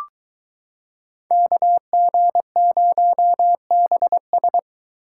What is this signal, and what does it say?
Morse code, heard as a keyed tone, about 22 wpm, 700 Hz. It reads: KG0BS